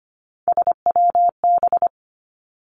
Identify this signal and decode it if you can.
Morse code, heard as a keyed tone, about 25 wpm, 700 Hz. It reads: SWB
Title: Three Challenging Letter-Number Combinations 25wpm